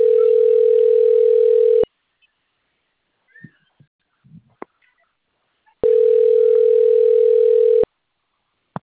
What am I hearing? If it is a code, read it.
Ringback tone (North America), 440 + 480 Hz.